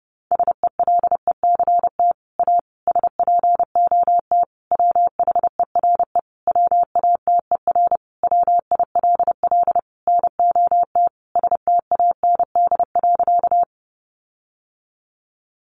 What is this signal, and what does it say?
Morse code, heard as a keyed tone, about 30 wpm, 700 Hz. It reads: SELECT A SPOT WHERE WATER WILL NOT STAND.
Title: Sentences from Top 1300 Words 30wpm